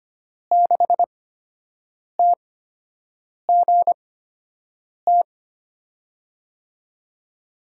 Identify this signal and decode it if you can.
Morse code, heard as a keyed tone, about 25 wpm, spaced as for 7 wpm, 700 Hz. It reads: 6TGT
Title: USA Call Signs - 5 Characters - 8x Character Spacing - 25wpm